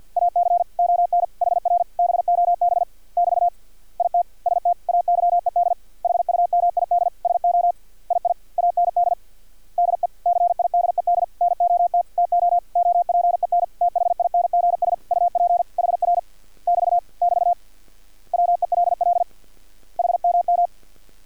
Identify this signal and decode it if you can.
Morse code, heard as a keyed tone, about 50 wpm, 700 Hz. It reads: AC ON HR BOB <BT> IT STAYED HUMID SO II AND DECIDED NOT TO OPEN THINGS UP HR <BT> <BT> WELL HMM
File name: archived / CRSnet